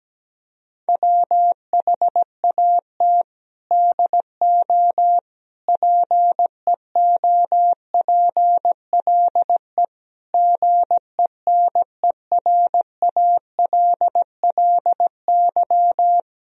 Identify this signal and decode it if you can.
Morse code, heard as a keyed tone, about 17 wpm, 700 Hz. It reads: WHAT DO PEOPLE GENERALLY
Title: Sentences from Top 1000 Words - Review Part 4 - 17wpm